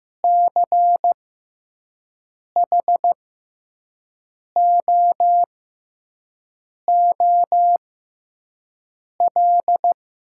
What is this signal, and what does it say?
Morse code, heard as a keyed tone, about 15 wpm, spaced as for 5 wpm, 700 Hz. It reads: CHOOL
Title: 6x Character Spacing - Top 300 Words - 15wpm